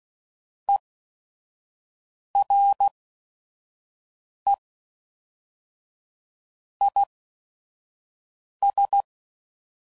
Morse code, heard as a keyed tone, about 16 wpm, spaced as for 5 wpm, 800 Hz. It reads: ERE IS
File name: CW Code Groups @5WPM / Technical Subjects / AND Gate